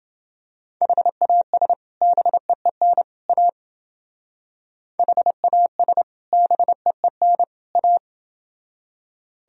Morse code, heard as a keyed tone, about 30 wpm, 700 Hz. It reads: HAS BEEN A HAS BEEN A